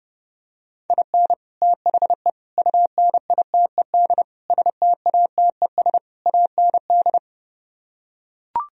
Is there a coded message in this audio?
Yes: Morse code from 0.9 to 7.2 s, heard as a keyed tone, about 32 wpm, 700 Hz. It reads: IN THE UNITED STATES AND